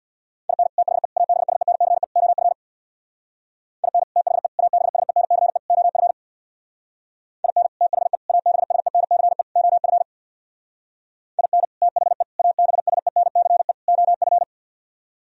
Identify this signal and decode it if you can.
Morse code, heard as a keyed tone, about 64 wpm, 700 Hz. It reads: IN THE ABSENCE OF IN THE ABSENCE OF IN THE ABSENCE OF IN THE ABSENCE OF